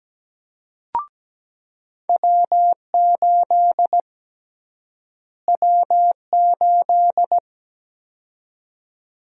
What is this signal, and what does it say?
Morse code, heard as a keyed tone, about 17 wpm, 700 Hz. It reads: W8 W8
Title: USA Call Sign Prefixes - Repeated 2x - 17wpm